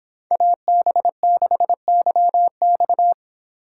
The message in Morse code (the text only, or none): AB6YX